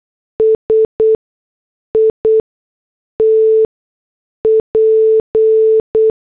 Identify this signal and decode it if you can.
Morse code, heard as a keyed tone, about 8 wpm, 440 Hz. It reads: S I T P
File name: morse / headlines